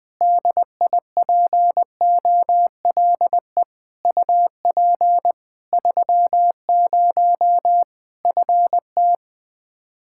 Morse code, heard as a keyed tone, about 20 wpm, 700 Hz. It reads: DIPOLE UP 30 FT
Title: Practice QSO - Exchange 2 - Info - 20wpm